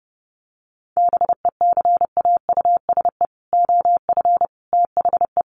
Morse code, heard as a keyed tone, about 30 wpm, 700 Hz. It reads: BECAUSE OF THE
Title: Sets of 3 Words 30wpm